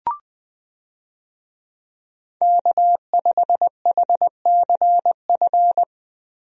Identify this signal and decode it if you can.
Morse code, heard as a keyed tone, about 21 wpm, 700 Hz. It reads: K5HCF
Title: USA Call Signs - 5 Characters - Repeated 3x - 20wpm